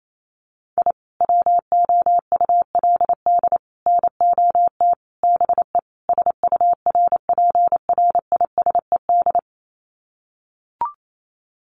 Morse code, heard as a keyed tone, about 28 wpm, 700 Hz. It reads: I WOULD NOT BE SURPRISED
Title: Sets of 5 Words Encore2 28wpm